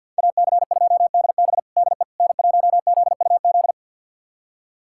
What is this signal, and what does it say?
Morse code, heard as a keyed tone, about 47 wpm, 700 Hz. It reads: AC2DB DE N1ZUZ
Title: USA Callsign DE Callsign - 5 Characters - 50wpm